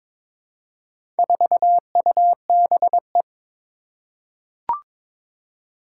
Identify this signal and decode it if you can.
Morse code, heard as a keyed tone, about 22 wpm, 700 Hz. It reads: 4UBE
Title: Four Challenging Letter-Number Combinations 22wpm